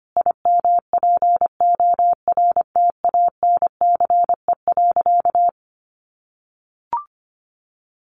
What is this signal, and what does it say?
Morse code, heard as a keyed tone, about 25 wpm, 700 Hz. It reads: IMPORTANCE.
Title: Sentences from Top 1200 Words 25wpm